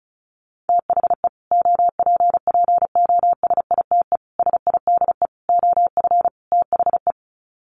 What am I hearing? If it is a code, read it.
Morse code, heard as a keyed tone, about 34 wpm, 700 Hz. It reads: THE OPPOSITE SIDE OF THE
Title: Sets of 5 Words Encore 35wpm